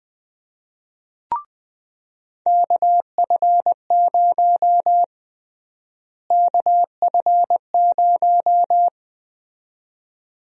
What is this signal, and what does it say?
Morse code, heard as a keyed tone, about 20 wpm, 700 Hz. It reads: KF0 KF0
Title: USA Call Sign Prefixes - Repeated 2x - 20wpm